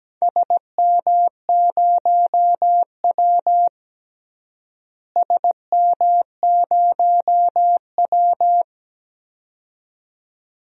Morse code, heard as a keyed tone, about 17 wpm, 700 Hz. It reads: SM0W SM0W
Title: DXCC-100 Call Signs - 4 Characters - Repeated 2x - 17wpm